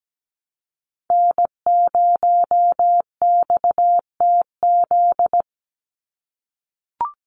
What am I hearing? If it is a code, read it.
Morse code, heard as a keyed tone, about 17 wpm, 700 Hz. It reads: N0XTZ
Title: USA Call Signs 17wpm